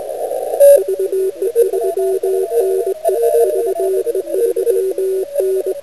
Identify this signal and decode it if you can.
Morse code, heard as a keyed tone, about 21 wpm, 380 Hz, with radio noise. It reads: VE3NE VE3N